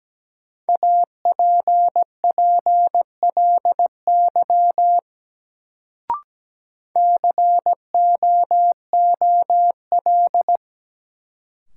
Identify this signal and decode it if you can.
Morse code, heard as a keyed tone, about 17 wpm, 700 Hz. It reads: APPLY COOL
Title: Sentences from Top 1200 Words - New Word Drill - 17wpm